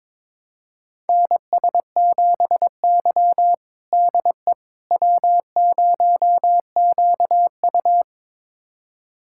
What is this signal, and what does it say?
Morse code, heard as a keyed tone, about 21 wpm, 700 Hz. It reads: NS7Y DE W0QU
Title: USA Callsign DE Callsign - 4 Characters - 22wpm